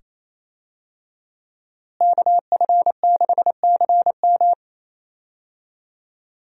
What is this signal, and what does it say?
Morse code, heard as a keyed tone, about 28 wpm, 700 Hz. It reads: KF6CM
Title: USA Call Signs - 5 Characters - Repeated 2x - 28wpm